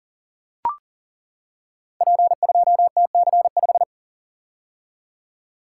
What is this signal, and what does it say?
Morse code, heard as a keyed tone, about 41 wpm, 690 Hz. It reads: P2TC5
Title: Code Groups - 5 Letter:Numbers - 40wpm